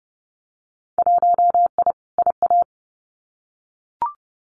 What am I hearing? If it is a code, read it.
Morse code, heard as a keyed tone, about 30 wpm, 700 Hz. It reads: 1I IA
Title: Winter Field Day Exchanges 30wpm